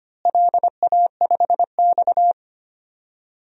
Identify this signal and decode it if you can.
Morse code, heard as a keyed tone, about 25 wpm, 700 Hz. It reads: LA5X
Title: DXCC-100 Call Signs - 4 Characters - Repeated 3x - 25wpm